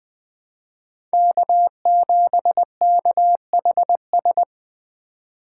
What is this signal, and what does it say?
Morse code, heard as a keyed tone, about 20 wpm, 700 Hz. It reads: K7KHS
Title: USA Call Signs - 5 Characters - 2x Character Spacing - 20wpm